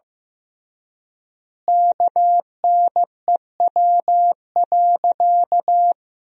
Morse code, heard as a keyed tone, about 15 wpm, 700 Hz. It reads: KNEW.
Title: 3x Word-Spacing - Sentences from Top 400 Words - 15wpm